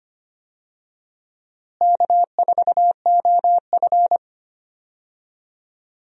Morse code, heard as a keyed tone, about 25 wpm, 700 Hz. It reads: K4OF
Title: USA Call Signs - 4 Characters - Repeated 2x - 25wpm